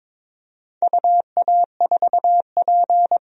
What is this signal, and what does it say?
Morse code, heard as a keyed tone, about 22 wpm, 700 Hz. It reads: UA4P
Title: DXCC-100 Call Signs - 4 Characters - 22wpm